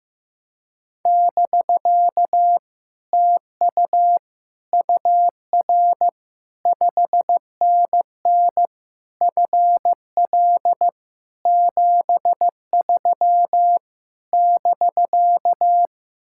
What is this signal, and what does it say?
Morse code, heard as a keyed tone, about 15 wpm, 700 Hz. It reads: <BK> TU UR 5NN FL 73 <BK>
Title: Speed-Racing Common QSO Phrases Encore - 15wpm